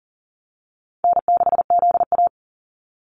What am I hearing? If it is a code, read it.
Morse code, heard as a keyed tone, about 42 wpm, 700 Hz. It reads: N6ZA
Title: NA SOTA Activator Callsigns 40wpm